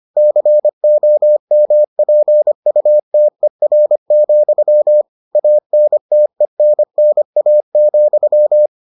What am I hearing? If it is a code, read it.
Morse code, heard as a keyed tone, about 25 wpm, 600 Hz. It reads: COMPUTER, ANTENNA,